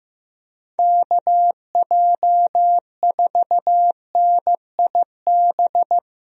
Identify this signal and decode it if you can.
Morse code, heard as a keyed tone, about 15 wpm, 700 Hz. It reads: KJ4NIB
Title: USA Call Signs - 6 Characters - Repeated 3x - 15wpm